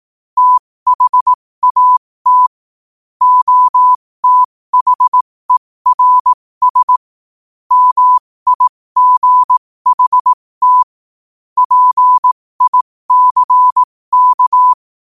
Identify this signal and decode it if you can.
Morse code, heard as a keyed tone, about 18 wpm, spaced as for 15 wpm, 1000 Hz. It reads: THAT OTHERS MIGHT PICK
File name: qotdfiles / 15wpm